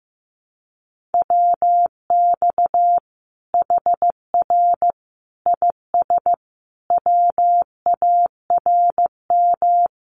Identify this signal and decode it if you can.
Morse code, heard as a keyed tone, about 15 wpm, 700 Hz. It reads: WX HR IS WARM
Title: Common QSO Phrases 15wpm